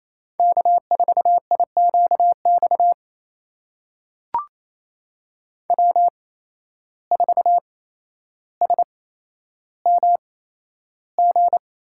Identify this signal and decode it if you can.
Morse code, heard as a keyed tone, about 28 wpm, spaced as for 12 wpm, 700 Hz. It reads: K4IQX W4SMG
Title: USA Call Signs - 5 Characters - 8x Character Spacing - 28wpm